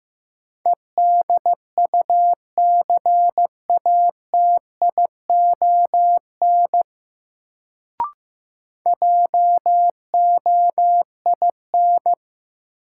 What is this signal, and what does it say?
Morse code, heard as a keyed tone, about 15 wpm, 700 Hz. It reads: EDUCATION JOIN
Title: Sentences from Top 1000 Words - Review Part 6 - New Word Drill - 15wpm